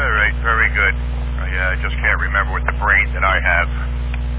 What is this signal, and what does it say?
Emergency Alert System (SAME) tones, heard as harsh buzzing bursts, with speech. It is heard from 0.5 to 3.7 s.